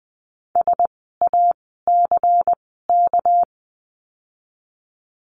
Morse code, heard as a keyed tone, about 20 wpm, spaced as for 15 wpm, 700 Hz. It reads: SACK
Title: 199 Lesson 32f - Letter k - Words 20wpm